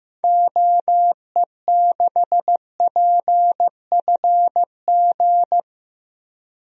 Morse code, heard as a keyed tone, about 16 wpm, 700 Hz. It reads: OE6PFG
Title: DXCC-100 Call Signs - 6 Characters - 15wpm